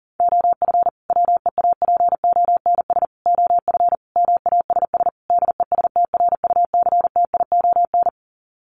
Morse code, heard as a keyed tone, about 40 wpm, 700 Hz. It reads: OF WEAPONS OF MASS DESTRUCTION
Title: Sets of 5 Words Encore 40wpm